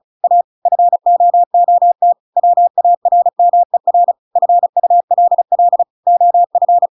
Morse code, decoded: A FOOT WARMER FULL OF

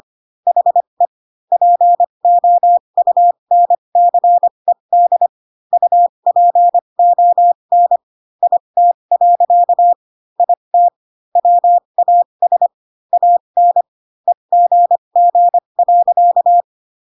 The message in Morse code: HE POUNCED UPON IT. IT WAS AN EGG.

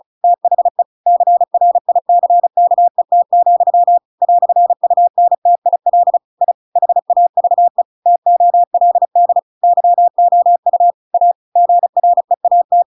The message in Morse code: THE CRICKET, "UNTIL I HAVE TOLD YOU A GREAT

